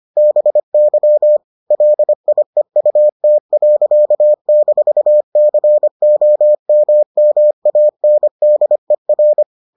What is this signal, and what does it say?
Morse code, heard as a keyed tone, about 25 wpm, 600 Hz. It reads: BY LIEUT.-COMMANDER